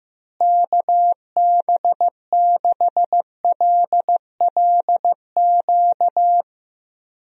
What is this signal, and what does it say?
Morse code, heard as a keyed tone, about 16 wpm, 700 Hz. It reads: KB6LLQ
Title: USA Call Signs - 6 Characters - Repeated 3x - 15wpm